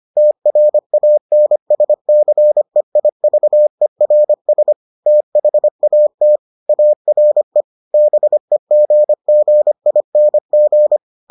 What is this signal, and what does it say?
Morse code, heard as a keyed tone, about 25 wpm, 600 Hz. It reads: TRANSCEIVERS THAT ARE BEGGING